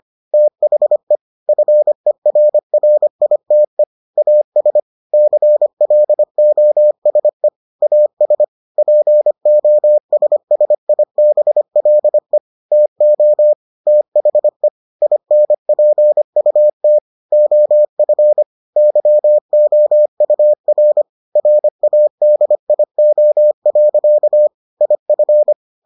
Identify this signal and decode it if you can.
Morse code, heard as a keyed tone, about 25 wpm, 600 Hz. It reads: THE FERRITE AS CLOSE AS POSSIBLE TO THE INPUT OF YOUR RADIO. IF